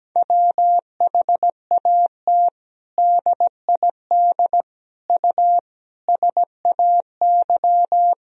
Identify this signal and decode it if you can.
Morse code, heard as a keyed tone, about 17 wpm, 700 Hz. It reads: WHAT DID U SAY
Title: Sentences from Top 200 Words and Abbreviations 17wpm